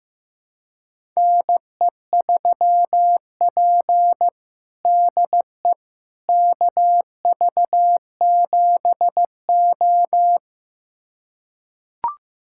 Morse code, heard as a keyed tone, about 15 wpm, 700 Hz. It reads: NE3P DE KV7O
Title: USA Callsign DE Callsign - 4 Characters - 15wpm